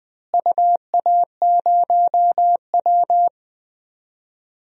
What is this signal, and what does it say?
Morse code, heard as a keyed tone, about 20 wpm, 700 Hz. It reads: UA0W
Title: DXCC-100 Call Signs - 4 Characters - Repeated 3x - 20wpm